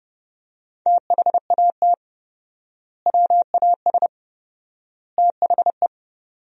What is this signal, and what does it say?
Morse code, heard as a keyed tone, about 29 wpm, 700 Hz. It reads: THAT WAS THE
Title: 4x Word-Spacing - Sentences from Top 400 Words - 30wpm